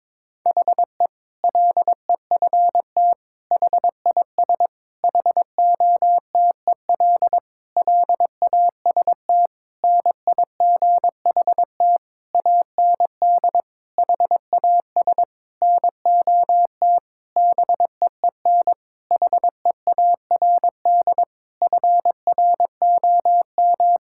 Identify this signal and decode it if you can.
Morse code, heard as a keyed tone, about 22 wpm, 700 Hz. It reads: HE LEFT HIS HOTEL LAST NIGHT AND HAS NOT BEEN HEARD FROM
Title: Sentences from Top 1600 Words 22wpm